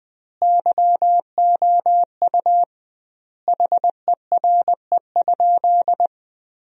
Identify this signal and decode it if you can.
Morse code, heard as a keyed tone, about 21 wpm, 700 Hz. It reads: YOU HERE?
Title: 2x Word-Spacing - Sentences from Top 200 Words - 20wpm